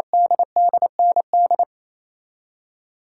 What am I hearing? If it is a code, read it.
Morse code, heard as a keyed tone, about 27 wpm, 700 Hz. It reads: DDND
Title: Code Groups - 5 Letters - 28wpm